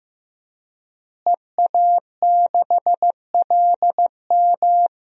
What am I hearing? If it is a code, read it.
Morse code, heard as a keyed tone, about 15 wpm, 700 Hz. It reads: EA6LM
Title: DXCC-100 Call Signs - 5 Characters - Repeated 3x - 15wpm